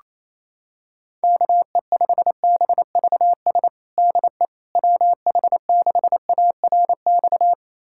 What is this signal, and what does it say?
Morse code, heard as a keyed tone, about 28 wpm, 700 Hz. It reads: KE5BVS DE WH6ARX
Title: USA Callsign DE Callsign - 6 Characters - 28wpm